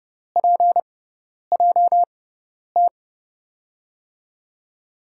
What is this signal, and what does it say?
Morse code, heard as a keyed tone, about 30 wpm, spaced as for 10 wpm, 700 Hz. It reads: PJT
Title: USA Call Signs - 6x Character Spacing - 30wpm